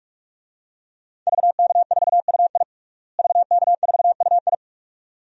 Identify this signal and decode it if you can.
Morse code, heard as a keyed tone, about 45 wpm, 700 Hz. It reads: VK4UI VK4UI
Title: DXCC-100 Call Signs - 5 Characters - Repeated 2x - 45wpm